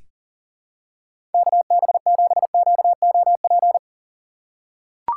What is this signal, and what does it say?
Morse code, heard as a keyed tone, about 40 wpm, 700 Hz. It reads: KB7QOP
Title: K1USN SST QSO Practice - 40wpm